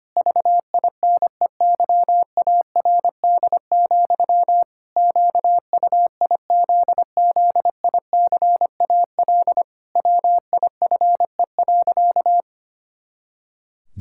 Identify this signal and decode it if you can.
Morse code, heard as a keyed tone, about 25 wpm, 700 Hz. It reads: VINEYARD, QUIZZICAL WIFE.
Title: Pangram Challenge v2 - 25wpm